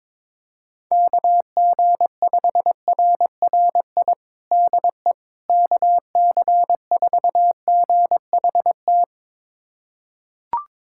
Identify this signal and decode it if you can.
Morse code, heard as a keyed tone, about 22 wpm, 700 Hz. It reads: KG5RRI DE KC4GHT
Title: USA Callsign DE Callsign - 6 Characters - 22wpm